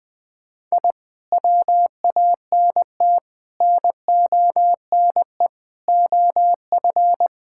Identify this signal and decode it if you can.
Morse code, heard as a keyed tone, about 21 wpm, 700 Hz. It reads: I WANT NONE OF
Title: Sentences with 2-Word Phrases - 20wpm